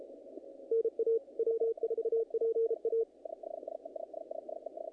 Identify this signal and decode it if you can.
Morse code, heard as a keyed tone, about 32 wpm, 450 Hz, with radio noise. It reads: NA W4PA
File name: contests / naqp-cw-k5tr-op-k5pi- -2000